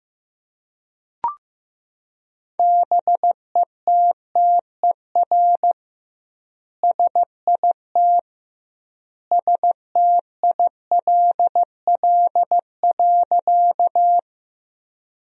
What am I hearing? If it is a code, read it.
Morse code, heard as a keyed tone, about 15 wpm, 700 Hz. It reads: BETTER SIT STILL.
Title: 2x Word-Spacing - Sentences from Top 500 Words - 15wpm